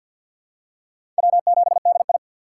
Morse code, heard as a keyed tone, about 46 wpm, 700 Hz. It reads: W7DI
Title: USA Call Signs - 4 Characters - 6x Character Spacing - 50wpm